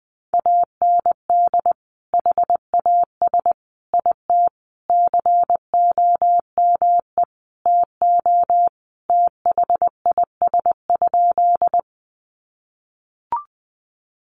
Morse code, heard as a keyed tone, about 20 wpm, 700 Hz. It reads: AND HAS IT COME TO THIS?
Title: Sentences from Top 100 Words 20wpm